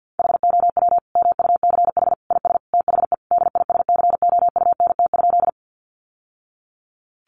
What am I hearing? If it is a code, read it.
Morse code, heard as a keyed tone, about 48 wpm, 700 Hz. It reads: HOW MUCH IS THE DISCOUNT?
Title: Sentences from Top 1200 Words 50wpm